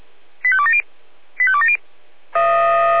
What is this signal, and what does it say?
Five-tone selective call (ZVEI), heard as quick runs of five tones, tones of about 70 ms. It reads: 85270 85270
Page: SDR/ZVEI